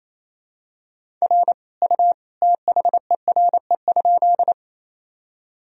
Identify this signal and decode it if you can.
Morse code, heard as a keyed tone, about 28 wpm, 700 Hz. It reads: R U THERE?
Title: Sentences from Top 100 Words and Abbreviations 28wpm